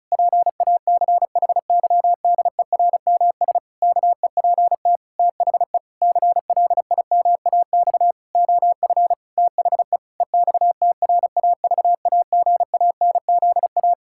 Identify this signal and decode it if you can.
Morse code, heard as a keyed tone, about 36 wpm, 700 Hz. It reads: PACHYDERMS KEPT THE CLIMAX OF THE EXTRAVAGANZA
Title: Pangram Challenge v2 - 35wpm